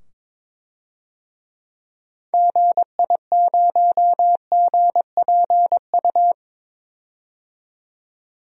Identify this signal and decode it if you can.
Morse code, heard as a keyed tone, about 22 wpm, 700 Hz. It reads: GI0GPU
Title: DXCC-100 Call Signs - 6 Characters - 22wpm